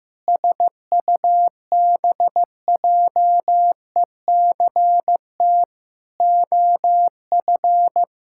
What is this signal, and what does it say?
Morse code, heard as a keyed tone, about 16 wpm, 700 Hz. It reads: SUBJECT OF
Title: Sentences from Top 1600 Words 15wpm